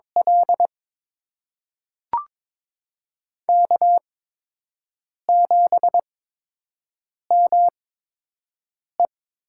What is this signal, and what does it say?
Morse code, heard as a keyed tone, about 22 wpm, spaced as for 6 wpm, 700 Hz. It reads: L K7ME
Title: USA Call Signs - 4 Characters - 8x Character Spacing - 22wpm